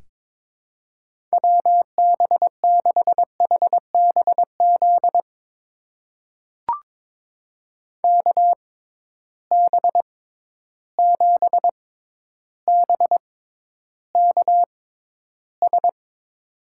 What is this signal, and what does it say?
Morse code, heard as a keyed tone, about 23 wpm, spaced as for 11 wpm, 700 Hz. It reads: WB6HBZ KB7BKS
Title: USA Call Signs - 6 Characters - 6x Character Spacing - 22wpm